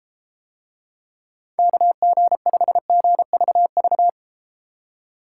Morse code, heard as a keyed tone, about 33 wpm, 700 Hz. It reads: KG5GVV